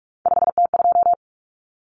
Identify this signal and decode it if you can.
Morse code, heard as a keyed tone, about 44 wpm, 700 Hz. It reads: 5T2
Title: Common Cut Numbers - 3-Digit 45wpm